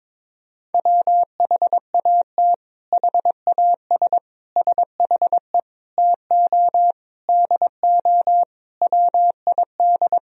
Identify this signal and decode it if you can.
Morse code, heard as a keyed tone, about 22 wpm, 700 Hz. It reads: WHAT HAS SHE TO DO WID
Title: Sentences from Top 100 Words and Abbreviations 22wpm